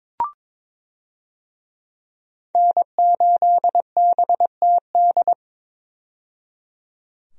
Morse code, heard as a keyed tone, about 22 wpm, 700 Hz. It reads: N8BTD
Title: USA Call Signs - 5 Characters - 22wpm